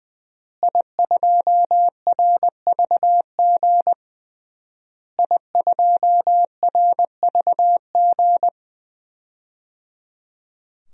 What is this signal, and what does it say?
Morse code, heard as a keyed tone, about 20 wpm, 700 Hz. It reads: I2RVG I2RVG
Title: DXCC-100 Call Signs - 5 Characters - Repeated 3x - 20wpm